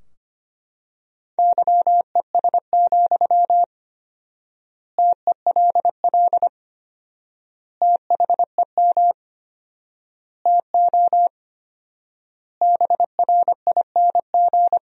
Morse code, heard as a keyed tone, about 25 wpm, 700 Hz. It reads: YES, TELL THEM TO BRING
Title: 4x Word-Spacing - Sentences from Top 500 Words - 25wpm